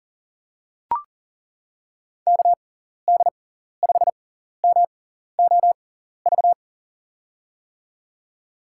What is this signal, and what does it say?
Morse code, heard as a keyed tone, about 40 wpm, spaced as for 14 wpm, 700 Hz. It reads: KD5MOV
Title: USA Call Signs - 6 Characters - 6x Character Spacing - 40wpm